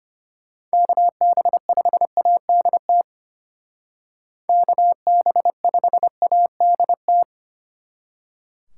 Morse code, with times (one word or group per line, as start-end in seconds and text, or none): 0.7-3.0 KB5ADT
4.5-7.2 KB5ADT